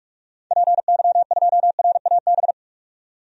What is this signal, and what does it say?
Morse code, heard as a keyed tone, about 42 wpm, 690 Hz. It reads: PY2RAB
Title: DXCC-100 Call Signs - 6 Characters - 45wpm